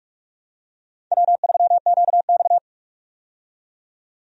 Morse code, heard as a keyed tone, about 43 wpm, 700 Hz. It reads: W3QX
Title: USA Call Signs - 4 Characters - 45wpm